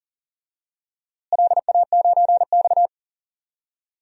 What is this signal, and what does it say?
Morse code, heard as a keyed tone, about 40 wpm, 700 Hz. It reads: LA9X
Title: DXCC-100 Call Signs - 4 Characters - Repeated 2x - 40wpm